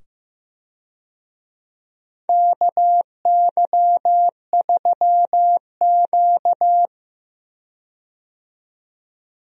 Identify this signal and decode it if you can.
Morse code, heard as a keyed tone, about 15 wpm, 700 Hz. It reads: KY3Q
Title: USA Call Signs - 4 Characters - 15wpm